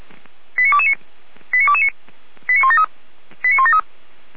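Five-tone selective call (ZVEI), heard as quick runs of five tones, tones of about 80 ms. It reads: 89229 89229 89173 89173
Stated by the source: ZVEI five-tone selective call received on an SDR, recorded by feuerwehr-orlamuende (fire brigade Orlamünde).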